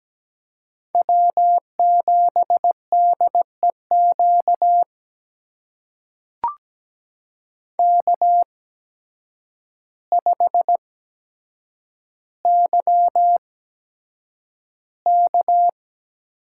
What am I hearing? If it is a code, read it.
Morse code, heard as a keyed tone, about 17 wpm, spaced as for 8 wpm, 700 Hz. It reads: W7DEQ K5YK
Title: USA Call Signs - 5 Characters - 8x Character Spacing - 17wpm